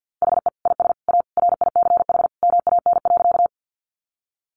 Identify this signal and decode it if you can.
Morse code, heard as a keyed tone, about 50 wpm, 700 Hz. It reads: HE IS A RICH MAN.
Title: Sentences from Top 1100 Words 50wpm